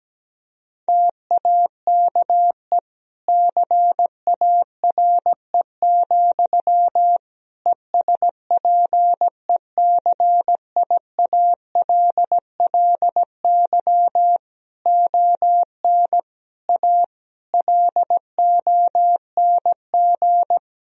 Morse code, heard as a keyed tone, about 17 wpm, 700 Hz. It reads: TAKE CARE, ESPECIALLY ON A LONG
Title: Sentences from Top 1000 Words - Review Part 4 - 17wpm